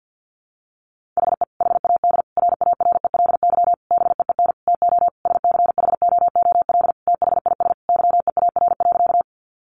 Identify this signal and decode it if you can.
Morse code, heard as a keyed tone, about 50 wpm, 700 Hz. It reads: HE HAD RARELY BEEN TO SCHOOL THIS YEAR.
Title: Sentences from Top 1400 Words 50wpm